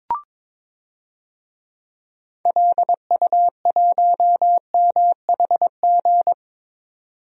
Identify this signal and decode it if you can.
Morse code, heard as a keyed tone, about 22 wpm, 700 Hz. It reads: LU1MHG
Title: DXCC-100 Call Signs - 6 Characters - 22wpm